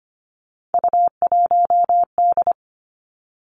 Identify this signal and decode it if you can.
Morse code, heard as a keyed tone, about 25 wpm, 700 Hz. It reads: U1D
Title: Three Challenging Letter-Number Combinations 25wpm